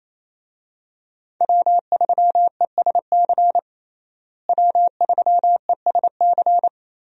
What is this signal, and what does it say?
Morse code, heard as a keyed tone, about 27 wpm, 700 Hz. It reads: W3ESC W3ESC
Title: USA Call Signs - 5 Characters - Repeated 2x - 28wpm